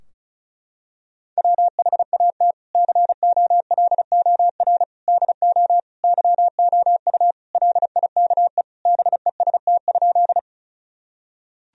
Morse code, heard as a keyed tone, about 34 wpm, 700 Hz. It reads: WHAT COLOR DO YOU LIKE BEST?
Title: Sentences from Top 1000 Words - Review Part 2 - 35wpm